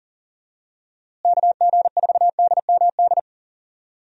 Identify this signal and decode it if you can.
Morse code, heard as a keyed tone, about 40 wpm, 700 Hz. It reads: KG4DMD
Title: USA Call Signs - 6 Characters - Repeated 3x - 40wpm